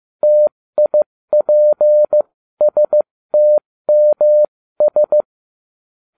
Morse code, heard as a keyed tone, about 15 wpm, spaced as for 13 wpm, 600 Hz. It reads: TIPSTMS